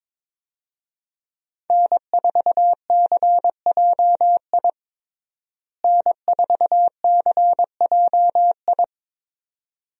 Morse code, heard as a keyed tone, about 22 wpm, 700 Hz. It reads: N4CJI N4CJI
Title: USA Call Signs - 5 Characters - Repeated 2x - 22wpm